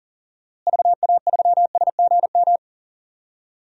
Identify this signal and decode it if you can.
Morse code, heard as a keyed tone, about 40 wpm, 700 Hz. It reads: VA3SGM
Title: DXCC-100 Call Signs - 6 Characters - Repeated 2x - 40wpm